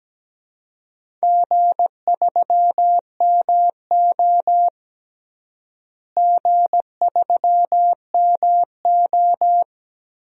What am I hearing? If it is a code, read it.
Morse code, heard as a keyed tone, about 17 wpm, 700 Hz. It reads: G3MO G3MO
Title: DXCC-100 Call Signs - 4 Characters - Repeated 3x - 17wpm